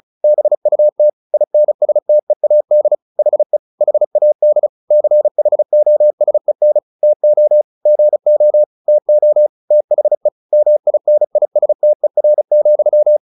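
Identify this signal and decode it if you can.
Morse code, heard as a keyed tone, about 34 wpm, 600 Hz. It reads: BUT INSTEAD HE HAD CHOSEN TO GO TO THE MINISTER,
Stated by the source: Morse practice at 35 wpm from CW med Gnister, sending Ken Follett's novel Never in English.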